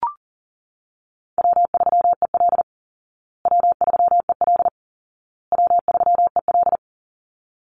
Morse code, heard as a keyed tone, about 40 wpm, 700 Hz. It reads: W3EL W3EL W3EL
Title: Easy Call Signs Repeated 3x - 40wpm